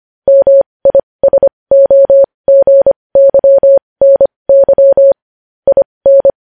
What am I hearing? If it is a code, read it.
Morse code, heard as a keyed tone, about 25 wpm, spaced as for 20 wpm, 550 Hz. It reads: MISOGYNY IN
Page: news-headlines-in-morse-code-at-20-wpm